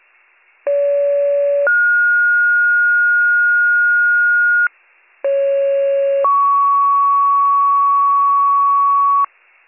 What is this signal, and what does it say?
Two-tone page, heard as a one-second tone, then a three-second tone, about 570 then 1430 Hz, about 550 then 1120 Hz, with noise. It is heard from 0.7 to 9.3 s.